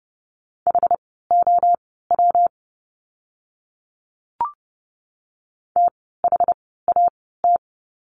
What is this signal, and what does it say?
Morse code, heard as a keyed tone, about 30 wpm, spaced as for 17 wpm, 700 Hz. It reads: HOW THAT
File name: course / 30:10wpm / 230 Lesson 37f - Letter x - Words 30wpm